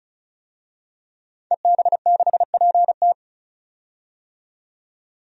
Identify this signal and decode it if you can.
Morse code, heard as a keyed tone, about 36 wpm, 700 Hz. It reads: EB6PT
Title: DXCC-100 Call Signs - 5 Characters - 35wpm